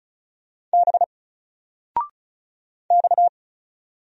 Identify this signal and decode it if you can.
Morse code, heard as a keyed tone, about 35 wpm, 700 Hz. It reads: B X
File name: Single-Letter-Number-35wpm